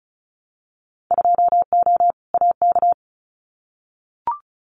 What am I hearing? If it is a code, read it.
Morse code, heard as a keyed tone, about 34 wpm, 700 Hz. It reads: 2O AK
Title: Winter Field Day Exchanges Encore 35wpm